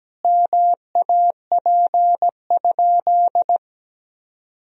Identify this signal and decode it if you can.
Morse code, heard as a keyed tone, about 17 wpm, 700 Hz. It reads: MAP?
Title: Sentences from Top 1000 Words - Review Part 2 - 17wpm